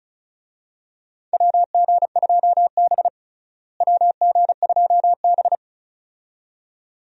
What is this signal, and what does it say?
Morse code, heard as a keyed tone, about 36 wpm, 700 Hz. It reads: WG2B WG2B
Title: USA Call Signs - 4 Characters - Repeated 2x - 35wpm